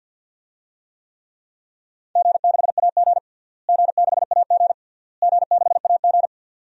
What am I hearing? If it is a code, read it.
Morse code, heard as a keyed tone, about 48 wpm, 700 Hz. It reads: G6AG G6AG G6AG
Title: DXCC-100 Call Signs - 4 Characters - Repeated 3x - 50wpm